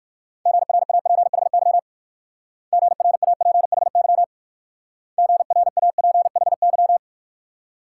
Morse code, heard as a keyed tone, about 54 wpm, 690 Hz. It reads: GRAPHY GRAPHY GRAPHY
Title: Speed-Racing Multi-letter Phonemes - 45wpm